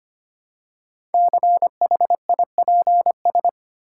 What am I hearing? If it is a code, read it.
Morse code, heard as a keyed tone, about 25 wpm, 700 Hz. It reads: CHIPS
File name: practice / 25wpm / Easy Words with Hints - 25wpm